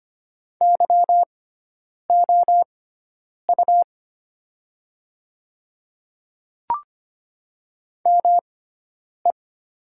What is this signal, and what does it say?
Morse code, heard as a keyed tone, about 25 wpm, spaced as for 9 wpm, 690 Hz. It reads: YOU ME
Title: 6x Character Spacing - Top 200 Words - 25wpm